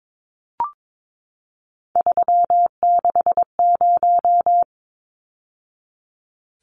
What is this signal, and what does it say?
Morse code, heard as a keyed tone, about 22 wpm, 700 Hz. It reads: 360
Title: Numbers - 3-Digit 22wpm